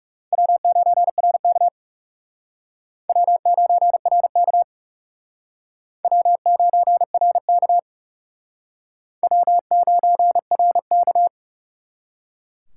Morse code, heard as a keyed tone, about 37 wpm, 700 Hz. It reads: W9RK W9RK W9RK W9RK
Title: Speed Racing - USA 1x2 Call Signs 30wpm